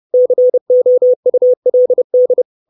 Morse code, heard as a keyed tone, about 30 wpm, 500 Hz. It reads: COULD
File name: 632 Words 30WPM